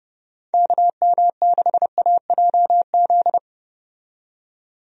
Morse code, heard as a keyed tone, about 30 wpm, 700 Hz. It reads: KM6AJZ
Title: USA Call Signs - 6 Characters - 30wpm